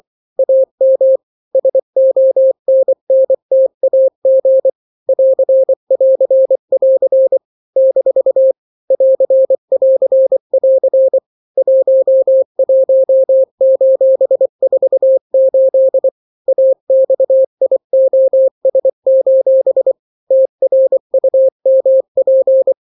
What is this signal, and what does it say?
Morse code, heard as a keyed tone, about 24 wpm, 530 Hz. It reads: AM SONNTAG <AR><AR><AR> - <AR><AR><AR> 11:48 AXIOS: TRUMP
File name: daily / 23wpm